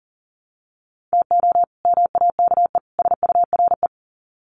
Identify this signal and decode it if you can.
Morse code, heard as a keyed tone, about 40 wpm, 700 Hz. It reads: TO MAKE SURE
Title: Sets of 3 Words 40wpm